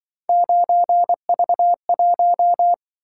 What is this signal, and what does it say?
Morse code, heard as a keyed tone, about 25 wpm, 700 Hz. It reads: 9V1
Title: Speed-Racing - Three Letter-Number Combinations - 20wpm